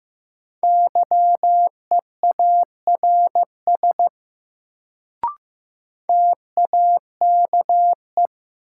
Morse code, heard as a keyed tone, about 15 wpm, 700 Hz. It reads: YEARS TAKE
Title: Sentences from Top 200 Words - New Word Drill - 15wpm